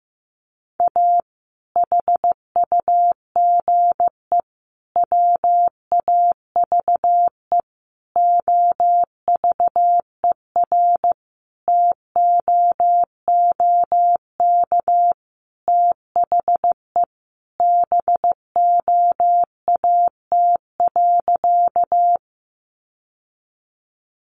Morse code, heard as a keyed tone, about 15 wpm, 700 Hz. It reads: A HUGE WAVE OVERTOOK THE BOAT.
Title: Sentences from Top 1400 Words 15wpm